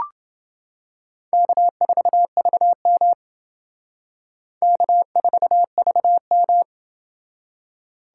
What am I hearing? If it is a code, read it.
Morse code, heard as a keyed tone, about 28 wpm, 700 Hz. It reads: K4VM K4VM